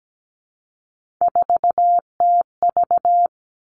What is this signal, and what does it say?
Morse code, heard as a keyed tone, about 17 wpm, 700 Hz. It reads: 4TV
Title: Three Challenging Letter-Number Combinations 17wpm